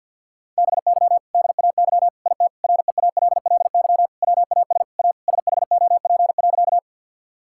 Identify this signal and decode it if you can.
Morse code, heard as a keyed tone, about 50 wpm, 700 Hz. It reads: BY DAY IT REALLY WAS A SHOP.
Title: Sentences from Top 1600 Words 50wpm